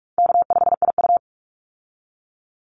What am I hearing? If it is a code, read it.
Morse code, heard as a keyed tone, about 44 wpm, 700 Hz. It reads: K5IU
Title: Easy Call Signs Repeated 2x - 45wpm